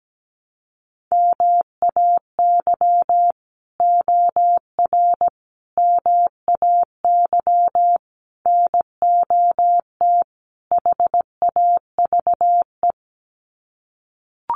Morse code, heard as a keyed tone, about 17 wpm, 700 Hz. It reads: MAY OR MAY NOT HAVE
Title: Sets of 5 Words Encore2 17wpm